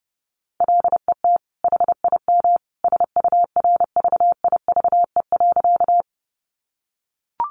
Morse code, heard as a keyed tone, about 30 wpm, 700 Hz. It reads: LET HIM SURVIVE.
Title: Sentences from Top 1300 Words 30wpm